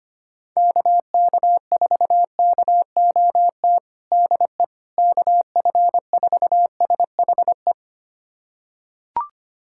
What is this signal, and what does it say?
Morse code, heard as a keyed tone, about 25 wpm, 700 Hz. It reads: KK4KOT DE KF4SHE
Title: USA Callsign DE Callsign 25wpm